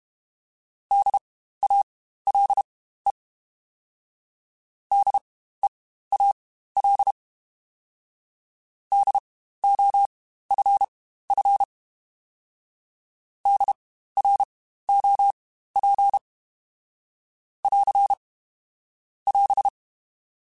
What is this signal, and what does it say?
Morse code, heard as a keyed tone, about 32 wpm, spaced as for 15 wpm, 800 Hz. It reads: DALE DEAL DOFF DROP <AR> <AS>
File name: Spaced Alphas / Spaced Alphas mp3s / Ds sp @30